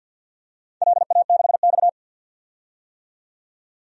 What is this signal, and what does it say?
Morse code, heard as a keyed tone, about 51 wpm, 700 Hz. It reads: LA6X
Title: DXCC-100 Call Signs - 4 Characters - 50wpm